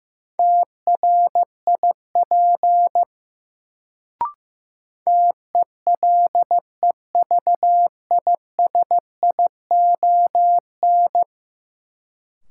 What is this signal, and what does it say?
Morse code, heard as a keyed tone, about 15 wpm, 700 Hz. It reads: TRIP TELEVISION
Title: Sentences from Top 1000 Words - Review Part 4 - New Word Drill - 15wpm